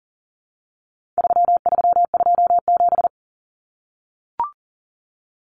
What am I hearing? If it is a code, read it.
Morse code, heard as a keyed tone, about 40 wpm, 700 Hz. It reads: 3327
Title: Common Cut Numbers - 4-Digit 40wpm